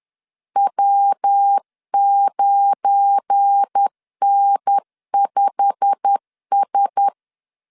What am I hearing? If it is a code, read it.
Morse code, heard as a keyed tone, about 10 wpm, 800 Hz. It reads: W9N5S
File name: LEZIONI / 8 YJ45 / 8 YJ45 10wpm 2549